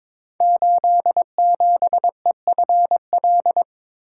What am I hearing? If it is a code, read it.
Morse code, heard as a keyed tone, about 22 wpm, 690 Hz. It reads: 87EFL
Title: Code Groups - 5 Letter:Numbers - 22wpm